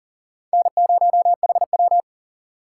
Morse code, heard as a keyed tone, about 54 wpm, spaced as for 47 wpm, 700 Hz. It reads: N0HW
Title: Speed Racing - USA 1x2 Call Signs 40wpm